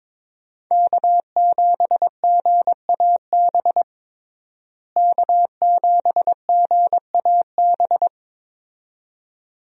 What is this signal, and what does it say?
Morse code, heard as a keyed tone, about 22 wpm, 700 Hz. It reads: K7GAB K7GAB
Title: USA Call Signs - 5 Characters - Repeated 2x - 22wpm